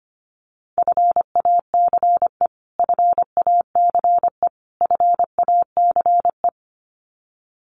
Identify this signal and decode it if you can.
Morse code, heard as a keyed tone, about 25 wpm, 700 Hz. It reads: FACE FACE FACE
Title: Top 500 Words Repeated 25wpm